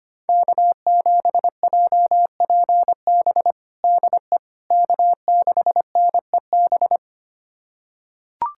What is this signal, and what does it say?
Morse code, heard as a keyed tone, about 25 wpm, 700 Hz. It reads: K7JPB DE K6NEB
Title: USA Callsign DE Callsign 25wpm